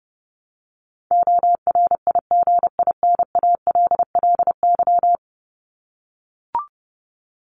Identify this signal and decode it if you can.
Morse code, heard as a keyed tone, about 30 wpm, 700 Hz. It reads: ORIGINALLY
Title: LY Suffix - Top 100 Words with Base Words 30wpm